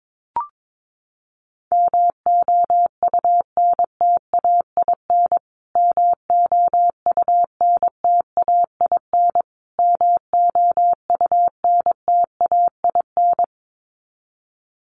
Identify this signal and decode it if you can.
Morse code, heard as a keyed tone, about 22 wpm, 700 Hz. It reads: MOUNTAIN MOUNTAIN MOUNTAIN
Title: Top 500 Words Repeated 22wpm